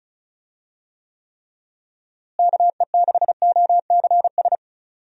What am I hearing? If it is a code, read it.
Morse code, heard as a keyed tone, about 34 wpm, 700 Hz. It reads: KE6OCS
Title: USA Call Signs - 6 Characters - Repeated 3x - 35wpm